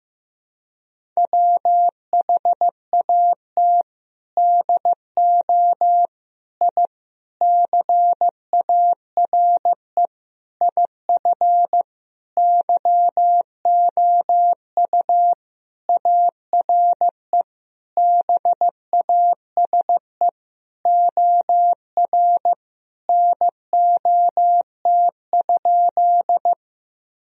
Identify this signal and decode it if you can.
Morse code, heard as a keyed tone, about 15 wpm, 700 Hz. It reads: WHAT DO I CARE IF YOU ARE BASE OR NOT?
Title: Sentences from Top 1000 Words - Review Part 1 - 15wpm